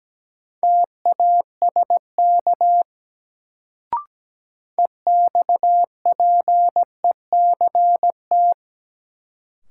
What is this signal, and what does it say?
Morse code, heard as a keyed tone, about 17 wpm, 700 Hz. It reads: TASK EXPECT
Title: Sentences from Top 1100 Words - New Word Drill Alt - 17wpm